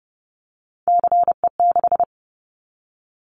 Morse code, heard as a keyed tone, about 30 wpm, 700 Hz. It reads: CE6
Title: Callsign Prefixes for Top 100 DXCC - 30wpm